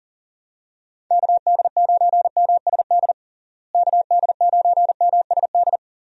Morse code, heard as a keyed tone, about 40 wpm, 700 Hz. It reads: KD9MSD KD9MSD
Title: USA Call Signs - 6 Characters - Repeated 2x - 40wpm